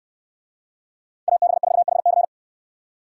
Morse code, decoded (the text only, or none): AB4HC